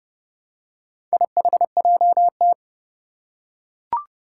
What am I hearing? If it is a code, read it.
Morse code, heard as a keyed tone, about 30 wpm, 700 Hz. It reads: IHJT
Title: Four Challenging Letter-Number Combinations 30wpm